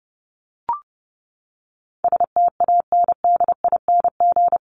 Morse code, heard as a keyed tone, about 30 wpm, 700 Hz. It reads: STANDING
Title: ING Suffix - Top 100 Words with Base Words 30wpm